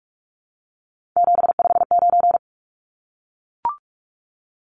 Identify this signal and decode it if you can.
Morse code, heard as a keyed tone, about 44 wpm, 700 Hz. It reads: 759
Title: Numbers - 3-Digit 45wpm